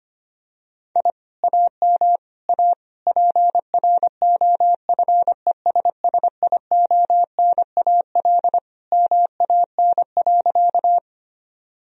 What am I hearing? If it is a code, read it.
Morse code, heard as a keyed tone, about 25 wpm, 700 Hz. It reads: I AM A PROFESSIONAL MAN.
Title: Sentences from Top 1000 Words - Review Part 4 - 25wpm